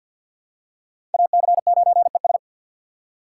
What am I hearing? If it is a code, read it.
Morse code, heard as a keyed tone, about 53 wpm, 700 Hz. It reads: AC9ES
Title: USA Call Signs - 5 Characters - 3x Character Spacing - 50wpm